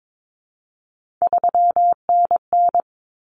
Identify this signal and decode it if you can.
Morse code, heard as a keyed tone, about 22 wpm, 700 Hz. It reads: 3NN
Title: Common Cut Numbers - 3-Digit 22wpm